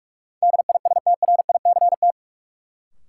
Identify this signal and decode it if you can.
Morse code, heard as a keyed tone, about 43 wpm, 700 Hz. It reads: DISTRICT
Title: Sentences from Top 1600 Words - New Word Drill Alt - 45wpm